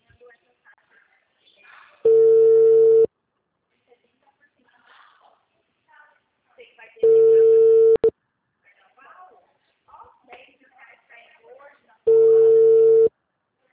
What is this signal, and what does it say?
Ringback tone (China, 450 Hz), with speech.